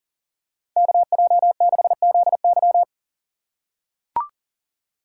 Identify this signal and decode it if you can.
Morse code, heard as a keyed tone, about 39 wpm, 700 Hz. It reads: KJ6ZY
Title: USA Call Signs - 5 Characters - 2x Character Spacing - 40wpm